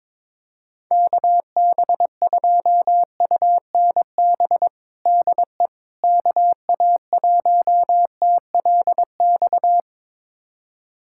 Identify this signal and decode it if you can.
Morse code, heard as a keyed tone, about 22 wpm, 700 Hz. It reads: KB2UNB DE KA1TLX
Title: USA Callsign DE Callsign - 6 Characters - 22wpm